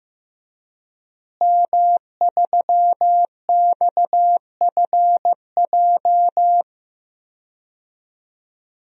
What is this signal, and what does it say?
Morse code, heard as a keyed tone, about 15 wpm, 700 Hz. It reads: M3XFJ
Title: DXCC-100 Call Signs - 5 Characters - Repeated 2x - 15wpm